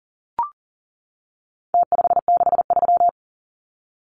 Morse code, heard as a keyed tone, about 40 wpm, 700 Hz. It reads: T563
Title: Numbers - Contest Style - 4-Digit 40wpm